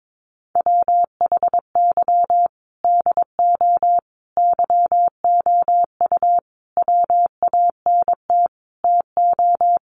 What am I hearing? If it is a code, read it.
Morse code, heard as a keyed tone, about 22 wpm, 700 Hz. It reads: WHY DO YOU WANT TO
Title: Sets of 5 Words Encore2 22wpm